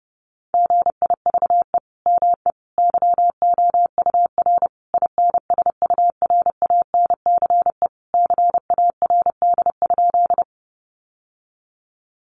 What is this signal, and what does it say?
Morse code, heard as a keyed tone, about 30 wpm, 700 Hz. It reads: GIVE ME YOUR INSURANCE CARD?
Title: Sentences from Top 1100 Words 30wpm